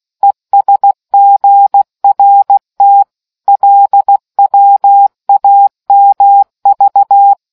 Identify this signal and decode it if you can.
Morse code, heard as a keyed tone, about 16 wpm, 800 Hz. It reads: ESGRT LWAMV